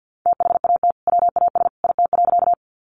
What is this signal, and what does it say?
Morse code, heard as a keyed tone, about 50 wpm, 700 Hz. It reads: THAT WAS IT.
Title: Sentences from Top 100 Words 50wpm